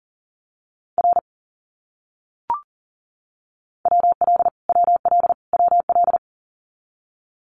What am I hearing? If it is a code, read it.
Morse code, heard as a keyed tone, about 38 wpm, 700 Hz. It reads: R WL WL WL
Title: Abbreviations Repeated 3x - 40wpm